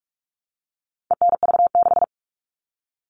Morse code, heard as a keyed tone, about 44 wpm, 700 Hz. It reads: ENV6
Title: All Cut Numbers - 4-Digit 45wpm